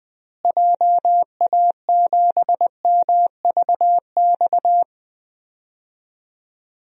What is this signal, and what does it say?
Morse code, heard as a keyed tone, about 20 wpm, 700 Hz. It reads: JA7MVX
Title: DXCC-100 Call Signs - 6 Characters - Repeated 3x - 20wpm